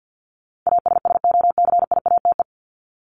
Morse code, heard as a keyed tone, about 50 wpm, 700 Hz. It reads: ASSOCIATE
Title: Top 1751-2000 Words 50wpm